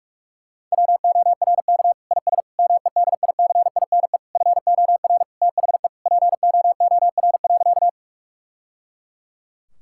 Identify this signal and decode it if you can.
Morse code, heard as a keyed tone, about 41 wpm, 700 Hz. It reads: WORK IS MEDICINE FOR THE POOR.
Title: Sentences from Top 1000 Words - Review Part 6 - 45wpm